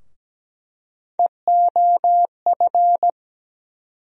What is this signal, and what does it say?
Morse code, heard as a keyed tone, about 17 wpm, 700 Hz. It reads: EOF
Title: USA Call Sign Postfixes - 3 Letters 17wpm